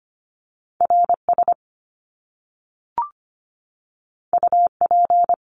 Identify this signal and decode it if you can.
Morse code, heard as a keyed tone, about 25 wpm, 700 Hz. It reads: R S U P
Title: Top Two Letter Combinations in Words 25wpm